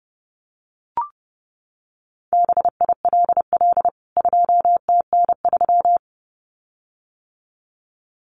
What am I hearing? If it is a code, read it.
Morse code, heard as a keyed tone, about 30 wpm, 700 Hz. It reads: BILL 2TN3
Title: CWT Exchange 30wpm